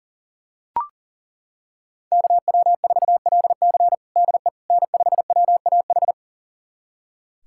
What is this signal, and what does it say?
Morse code, heard as a keyed tone, about 40 wpm, 700 Hz. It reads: KW4LC DE N5WAH
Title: USA Callsign DE Callsign - 5 Characters - 40wpm